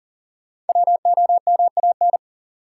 Morse code, heard as a keyed tone, about 39 wpm, 700 Hz. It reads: WOMAN
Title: Sentences from Top 1100 Words - New Word Drill Alt - 40wpm